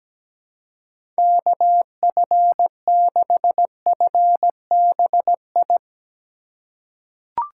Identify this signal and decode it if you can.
Morse code, heard as a keyed tone, about 17 wpm, 700 Hz. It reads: KF6FBI
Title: USA Call Signs - 6 Characters - 3x Character Spacing - 17wpm